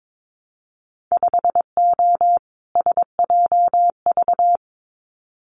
Morse code, heard as a keyed tone, about 22 wpm, 700 Hz. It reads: 5O SJV
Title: Winter Field Day Exchanges Encore 22wpm